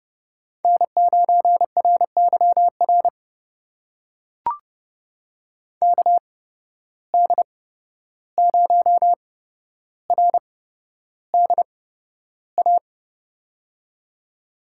Morse code, heard as a keyed tone, about 30 wpm, spaced as for 12 wpm, 700 Hz. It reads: N9RYR KD0RDA
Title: USA Call Signs - 8x Character Spacing - 30wpm